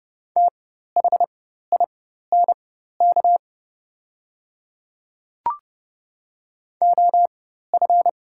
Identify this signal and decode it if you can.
Morse code, heard as a keyed tone, about 29 wpm, spaced as for 14 wpm, 700 Hz. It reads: THINK OF
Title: 4x Character Spacing - Top 200 Words - 30wpm